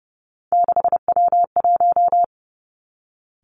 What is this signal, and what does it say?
Morse code, heard as a keyed tone, about 30 wpm, 700 Hz. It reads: 6W1
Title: Three Challenging Letter-Number Combinations 30wpm